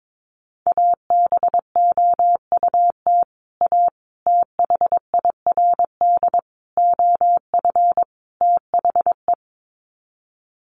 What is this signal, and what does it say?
Morse code, heard as a keyed tone, about 22 wpm, 700 Hz. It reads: ABOUT A THIRD OF THE
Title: Sets of 5 Words Encore2 22wpm